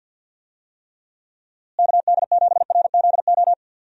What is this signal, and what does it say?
Morse code, heard as a keyed tone, about 47 wpm, 700 Hz. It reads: KD7RZO
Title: USA Call Signs - 6 Characters - Repeated 3x - 50wpm